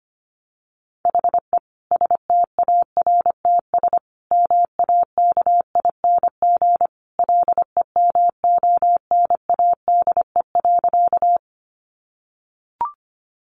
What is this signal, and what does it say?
Morse code, heard as a keyed tone, about 25 wpm, 700 Hz. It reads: HE STARTS MAKING LEMONADE.
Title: Additional Easy Phrases 25wpm